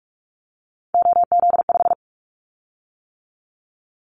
Morse code, heard as a keyed tone, about 46 wpm, 700 Hz. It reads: OZ5
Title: Callsign Prefixes for Top 50 DXCC - 45wpm